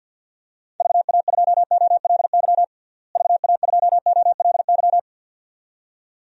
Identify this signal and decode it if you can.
Morse code, heard as a keyed tone, about 53 wpm, 700 Hz. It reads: VA2OLY VA2OLY
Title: DXCC-100 Call Signs - 6 Characters - Repeated 2x - 50wpm